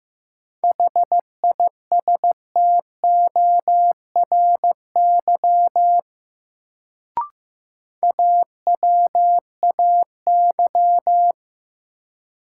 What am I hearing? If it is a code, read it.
Morse code, heard as a keyed tone, about 15 wpm, 700 Hz. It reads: HISTORY AWAY
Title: Sentences from Top 600 Words - New Word Drill - 15wpm